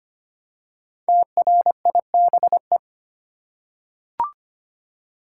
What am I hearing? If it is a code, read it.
Morse code, heard as a keyed tone, about 25 wpm, 700 Hz. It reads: TRIBE
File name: 5 Letter Words - First 3 Letters Given - 25wpm